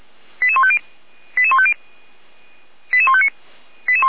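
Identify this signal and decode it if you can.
Five-tone selective call (ZVEI), heard as quick runs of five tones, tones of about 70 ms. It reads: 88150 88150 88159
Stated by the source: ZVEI five-tone selective call received on an SDR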